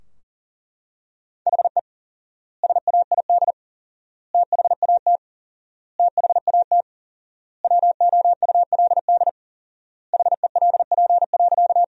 Morse code, heard as a keyed tone, about 40 wpm, 700 Hz. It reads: HE SAID THAT THAT WOULD HELP.
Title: 4x Word-Spacing - Sentences from Top 200 Words - 40wpm